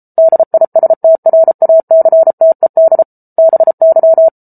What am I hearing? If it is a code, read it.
Morse code, heard as a keyed tone, about 34 wpm, 650 Hz. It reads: DISTRACTED BY